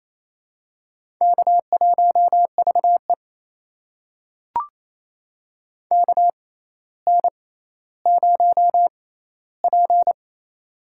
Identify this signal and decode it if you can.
Morse code, heard as a keyed tone, about 28 wpm, spaced as for 14 wpm, 700 Hz. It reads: K1VE KN0P
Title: USA Call Signs - 6x Character Spacing - 28wpm